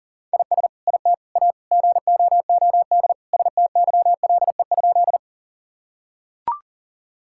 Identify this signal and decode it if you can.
Morse code, heard as a keyed tone, about 40 wpm, 700 Hz. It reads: IS IT A GOOD STYLE?
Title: Sentences from Top 1000 Words - Review Part 5 - 40wpm